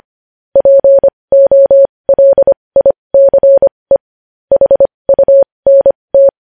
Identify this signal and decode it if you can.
Morse code, heard as a keyed tone, about 25 wpm, spaced as for 20 wpm, 550 Hz. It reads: POLICE HUNT
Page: news-headlines-in-morse-code-at-20-wpm